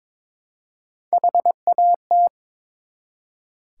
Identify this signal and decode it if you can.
Morse code, heard as a keyed tone, about 23 wpm, 700 Hz. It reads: HAT
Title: Three Letter Words - 22wpm